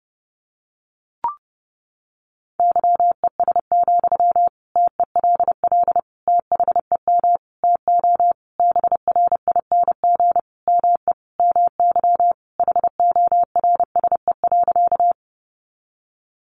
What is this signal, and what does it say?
Morse code, heard as a keyed tone, about 30 wpm, 700 Hz. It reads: YES, TELL THEM TO BRING ME MY HORSE.
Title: Sentences from Top 500 Words 30wpm